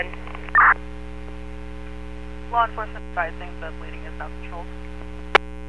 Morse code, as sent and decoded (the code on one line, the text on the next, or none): none